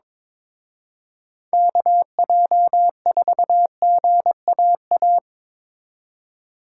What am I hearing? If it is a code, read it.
Morse code, heard as a keyed tone, about 25 wpm, 700 Hz. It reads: KJ4GAA
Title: Speed Racing - USA 2x3 Call Signs 22wpm